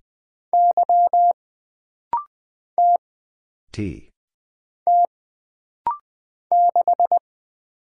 Morse code, heard as a keyed tone, about 20 wpm, 700 Hz. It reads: Y T T 6
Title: Single Letter-Number Alt - Rapid-Fire 20wpm